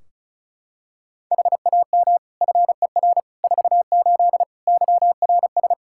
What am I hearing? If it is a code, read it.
Morse code, heard as a keyed tone, about 36 wpm, 700 Hz. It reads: HAM FER 48 YRS
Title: Practice QSO - Exchange 3 - More Info - 35wpm